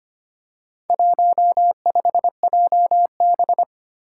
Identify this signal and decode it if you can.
Morse code, heard as a keyed tone, about 25 wpm, 700 Hz. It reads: 15JB
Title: Four Challenging Letter-Number Combinations 25wpm